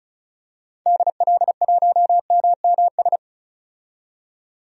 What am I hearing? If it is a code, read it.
Morse code, heard as a keyed tone, about 36 wpm, 690 Hz. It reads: DL1MMS
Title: DXCC-100 Call Signs - 6 Characters - 35wpm